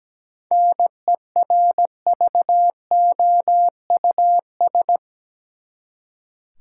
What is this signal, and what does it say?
Morse code, heard as a keyed tone, about 17 wpm, 700 Hz. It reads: NERVOUS
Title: Seven Letter Words - 17wpm